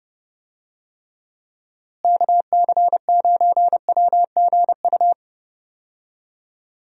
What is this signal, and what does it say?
Morse code, heard as a keyed tone, about 30 wpm, 700 Hz. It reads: KC9WGU
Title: USA Call Signs - 6 Characters - Repeated 3x - 30wpm